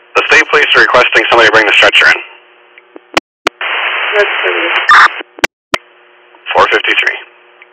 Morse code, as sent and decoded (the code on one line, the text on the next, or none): none